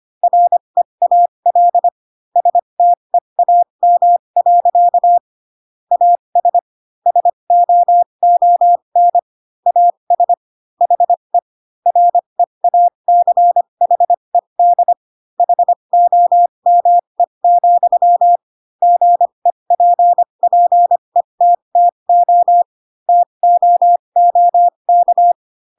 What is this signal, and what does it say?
Morse code, heard as a keyed tone, about 25 wpm, spaced as for 22 wpm, 700 Hz. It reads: REAL STEAM. AS SOON AS HE REACHED HOME, GEPPETTO TOOK
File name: Pinocchio / Pinocchio ch3 18 25 ik1hge